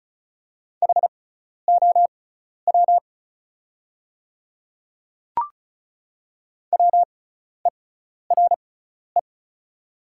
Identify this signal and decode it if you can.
Morse code, heard as a keyed tone, about 36 wpm, spaced as for 12 wpm, 690 Hz. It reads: HOW WERE
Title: 6x Character Spacing - Top 300 Words - 35wpm